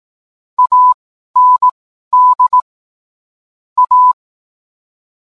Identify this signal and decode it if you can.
Morse code, heard as a keyed tone, about 18 wpm, spaced as for 12 wpm, 1000 Hz. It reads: AND A